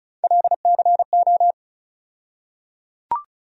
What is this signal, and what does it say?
Morse code, heard as a keyed tone, about 34 wpm, 700 Hz. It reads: LCO
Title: USA Call Sign Postfixes - 3 Letters 35wpm